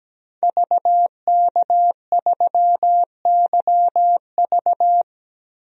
Morse code, heard as a keyed tone, about 17 wpm, 700 Hz. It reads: VK3YV
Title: DXCC-100 Call Signs - 5 Characters - 17wpm